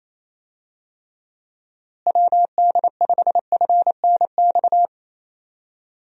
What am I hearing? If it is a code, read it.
Morse code, heard as a keyed tone, about 28 wpm, 700 Hz. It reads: WD5FNX
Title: USA Call Signs - 6 Characters - Repeated 2x - 28wpm